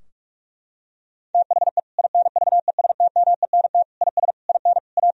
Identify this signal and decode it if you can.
Morse code, heard as a keyed tone, about 41 wpm, 700 Hz. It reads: THE INVESTMENT IS IN A
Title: Sentences from Top 1000 Words - Review Part 4 - 45wpm